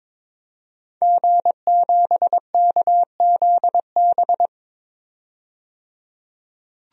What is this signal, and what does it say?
Morse code, heard as a keyed tone, about 22 wpm, 700 Hz. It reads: G7KZB
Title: DXCC-100 Call Signs - 5 Characters - 22wpm